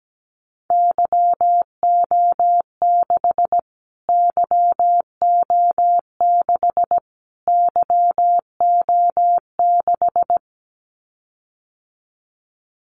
Morse code, heard as a keyed tone, about 17 wpm, 700 Hz. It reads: YO6 YO6 YO6
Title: Callsign Prefixes for Top 25 DXCC - Repeated 3x - 17wpm